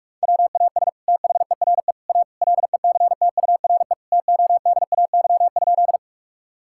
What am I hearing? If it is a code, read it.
Morse code, heard as a keyed tone, about 44 wpm, 700 Hz. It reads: WAS THERE A LECTURE TODAY?
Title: Sentences from Top 1600 Words 45wpm